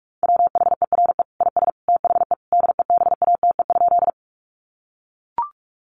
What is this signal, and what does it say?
Morse code, heard as a keyed tone, about 44 wpm, 700 Hz. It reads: WHERE IS THE DEBATE?
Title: Sentences from Top 1100 Words 45wpm